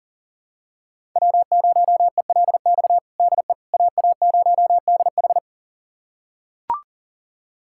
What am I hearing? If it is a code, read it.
Morse code, heard as a keyed tone, about 37 wpm, 700 Hz. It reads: W0ELX DE AA0DH
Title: USA Callsign DE Callsign - 5 Characters - 40wpm